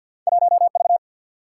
Morse code, heard as a keyed tone, about 48 wpm, 700 Hz. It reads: 1V
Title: Single-Letter Single-Number 50wpm